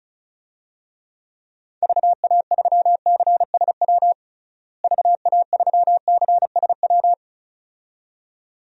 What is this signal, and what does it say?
Morse code, heard as a keyed tone, about 34 wpm, 700 Hz. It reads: VA3CSW VA3CSW
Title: DXCC-100 Call Signs - 6 Characters - Repeated 2x - 35wpm